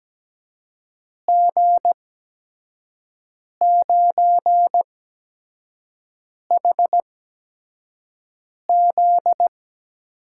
Morse code, heard as a keyed tone, about 17 wpm, spaced as for 5 wpm, 700 Hz. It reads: G9HZ
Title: USA Call Signs - 5 Characters - 8x Character Spacing - 17wpm